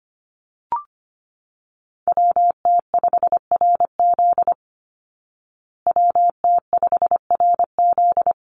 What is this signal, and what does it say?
Morse code, heard as a keyed tone, about 25 wpm, 700 Hz. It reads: WT5RZ WT5RZ
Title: Call Signs Repeated 3x - 25wpm